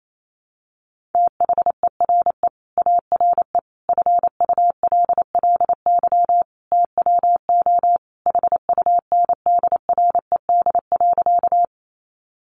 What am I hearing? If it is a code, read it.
Morse code, heard as a keyed tone, about 28 wpm, 700 Hz. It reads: THERE ARE FULLY TWO HUNDRED.
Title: Sentences from Top 1300 Words 28wpm